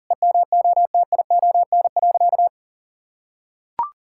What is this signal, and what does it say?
Morse code, heard as a keyed tone, about 40 wpm, 700 Hz. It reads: EMOTION.
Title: Sentences from Top 1600 Words 40wpm